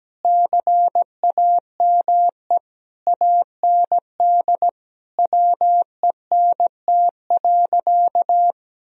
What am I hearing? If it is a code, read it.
Morse code, heard as a keyed tone, about 17 wpm, 700 Hz. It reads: CAME AND WENT.
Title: Sentences from Top 1000 Words - Review Part 2 - 17wpm